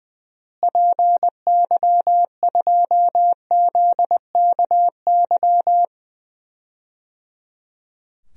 Morse code, heard as a keyed tone, about 21 wpm, 700 Hz. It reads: PY2ZKY
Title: DXCC-100 Call Signs - 6 Characters - Repeated 3x - 20wpm